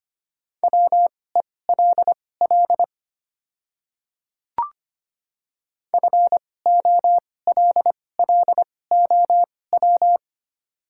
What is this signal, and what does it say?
Morse code, heard as a keyed tone, about 25 wpm, spaced as for 18 wpm, 700 Hz. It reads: WELL FOLLOW
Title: 2x Character Spacing - Top 300 Words - 25wpm